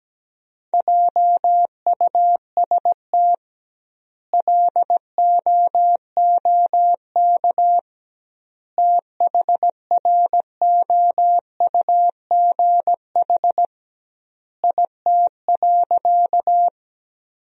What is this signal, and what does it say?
Morse code, heard as a keyed tone, about 17 wpm, 700 Hz. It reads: JUST LOOK THROUGH IT.
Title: 2x Word-Spacing - Sentences from Top 200 Words - 17wpm